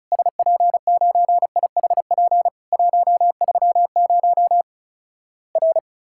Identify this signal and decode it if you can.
Morse code, heard as a keyed tone, about 36 wpm, 700 Hz. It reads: SP9IHP 130 R
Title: Worked All Europe DX Contest - QTC Exchange - 35wpm